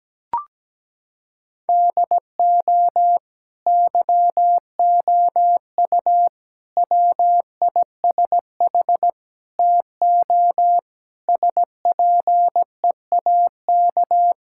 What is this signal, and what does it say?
Morse code, heard as a keyed tone, about 17 wpm, 700 Hz. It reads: DO YOU WISH TO SPEAK
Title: Sentences from Top 1600 Words 17wpm